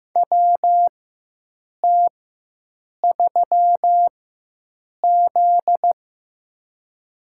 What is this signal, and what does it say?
Morse code, heard as a keyed tone, about 15 wpm, spaced as for 7 wpm, 700 Hz. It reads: WT3Z
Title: USA Call Signs - 4 Characters - 4x Character Spacing - 15wpm